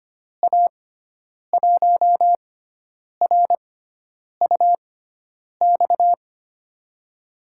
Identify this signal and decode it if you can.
Morse code, heard as a keyed tone, about 25 wpm, spaced as for 9 wpm, 700 Hz. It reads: A1RUX
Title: USA Call Signs - 6 Characters - 6x Character Spacing - 25wpm